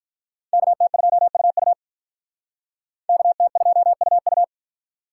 Morse code, heard as a keyed tone, about 45 wpm, 700 Hz. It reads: KT2UU KT2UU